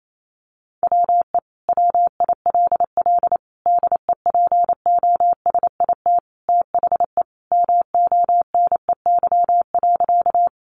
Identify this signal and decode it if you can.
Morse code, heard as a keyed tone, about 28 wpm, 700 Hz. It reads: WE WILL DEPOSIT THE MONEY.
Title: Sentences from Top 1300 Words 28wpm